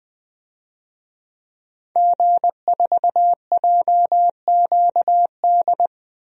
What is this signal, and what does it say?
Morse code, heard as a keyed tone, about 20 wpm, 700 Hz. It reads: G4JQD
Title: DXCC-100 Call Signs - 5 Characters - 20wpm